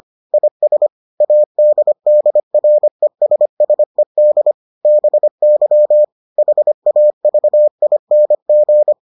Morse code, heard as a keyed tone, about 25 wpm, 600 Hz. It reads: IS ADDRESSED BY HAVING